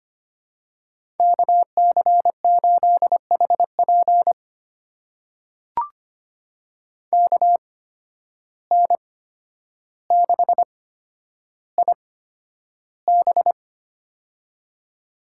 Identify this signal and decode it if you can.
Morse code, heard as a keyed tone, about 25 wpm, spaced as for 11 wpm, 700 Hz. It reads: KC8HP KN6IB
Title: USA Call Signs - 5 Characters - 8x Character Spacing - 25wpm